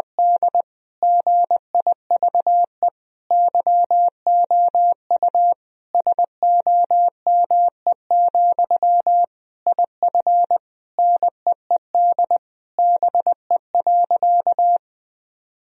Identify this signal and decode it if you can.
Morse code, heard as a keyed tone, about 20 wpm, 700 Hz. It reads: D GIVE YOU SOME, IF NEED BE.
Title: Sentences from Top 1000 Words - Review Part 1 - 20wpm